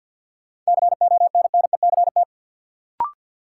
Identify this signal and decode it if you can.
Morse code, heard as a keyed tone, about 52 wpm, 700 Hz. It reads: CONNECT
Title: Sentences from Top 1500 Words - New Word Drill Alt - 50wpm